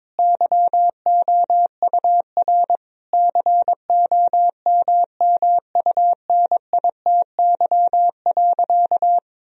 Morse code, heard as a keyed tone, about 22 wpm, 700 Hz. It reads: YOUR COMMUNITY.
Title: Sentences from Top 1000 Words - Review Part 4 - 22wpm